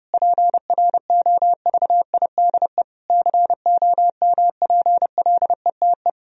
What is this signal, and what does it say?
Morse code, heard as a keyed tone, about 30 wpm, 700 Hz. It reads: PROVIDE COMPLETE
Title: Sentences from Top 1000 Words - Review Part 4 - 30wpm